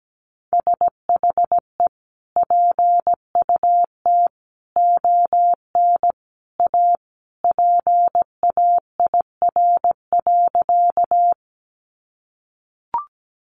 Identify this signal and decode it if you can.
Morse code, heard as a keyed tone, about 17 wpm, 700 Hz. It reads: SHE PUT ON A PAIR.
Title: Sentences from Top 1200 Words 17wpm